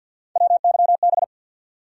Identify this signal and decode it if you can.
Morse code, heard as a keyed tone, about 51 wpm, 700 Hz. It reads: WYB
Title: USA Call Sign Postfixes - 3 Letters 50wpm